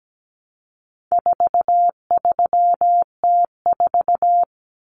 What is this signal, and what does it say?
Morse code, heard as a keyed tone, about 17 wpm, 700 Hz. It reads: 43T4
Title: Common Cut Numbers - 4-Digit 17wpm